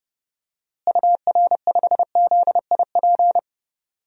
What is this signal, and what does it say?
Morse code, heard as a keyed tone, about 32 wpm, 700 Hz. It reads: UR5ZIP
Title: DXCC-100 Call Signs - 6 Characters - Repeated 3x - 30wpm